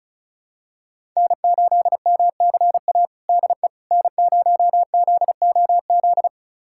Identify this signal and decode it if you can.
Morse code, heard as a keyed tone, about 34 wpm, 700 Hz. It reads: N8MCA DE N0ZOZ
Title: USA Callsign DE Callsign - 5 Characters - 35wpm